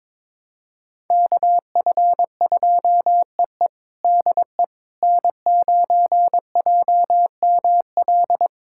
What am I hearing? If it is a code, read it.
Morse code, heard as a keyed tone, about 22 wpm, 700 Hz. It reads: KF2EE DE N9JML
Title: USA Callsign DE Callsign - 5 Characters - 22wpm